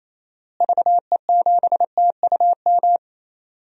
Morse code, heard as a keyed tone, about 28 wpm, 700 Hz. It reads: VE7TUM
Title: DXCC-100 Call Signs - 6 Characters - Repeated 2x - 28wpm